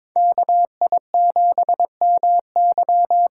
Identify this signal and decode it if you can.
Morse code, heard as a keyed tone, about 22 wpm, 700 Hz. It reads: KI7MY
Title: USA Call Signs - 6 Characters - 8x Character Spacing - 22wpm